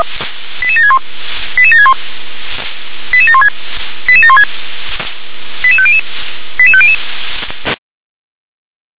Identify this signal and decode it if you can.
Five-tone selective call (ZVEI), heard as quick runs of five tones, tones of about 70 ms. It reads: 88761 88761 88716 88716 88500 88500
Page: SDR/ZVEI